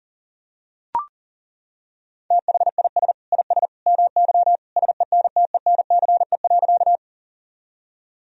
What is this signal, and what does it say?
Morse code, heard as a keyed tone, about 37 wpm, 700 Hz. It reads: THIS IS MY SENTENCE.
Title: Sentences from Top 1000 Words - Review Part 1 - 40wpm